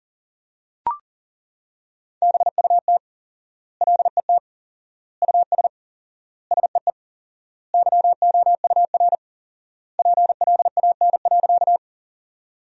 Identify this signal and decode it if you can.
Morse code, heard as a keyed tone, about 37 wpm, 700 Hz. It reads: BUT LET US SEE YOUR PLAN.
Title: 4x Word-Spacing - Sentences from Top 500 Words - 40wpm